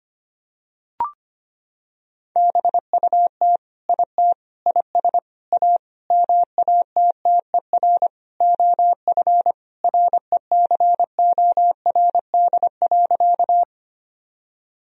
Morse code, heard as a keyed tone, about 25 wpm, 700 Hz. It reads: BUT IT IS A MATTER OF RECORD.
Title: Sentences from Top 1000 Words - Review Part 3 - 25wpm